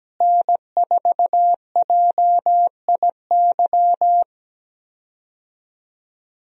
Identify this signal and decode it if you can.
Morse code, heard as a keyed tone, about 17 wpm, 700 Hz. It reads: N4JIY
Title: USA Call Signs - 5 Characters - Repeated 3x - 17wpm